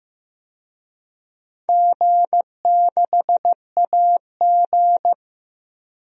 Morse code, heard as a keyed tone, about 15 wpm, 700 Hz. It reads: G6AG
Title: DXCC-100 Call Signs - 4 Characters - 15wpm